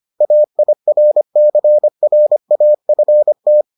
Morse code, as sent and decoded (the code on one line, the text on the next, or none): .- .. .-. -.-. .-. .- ..-. -
AIRCRAFT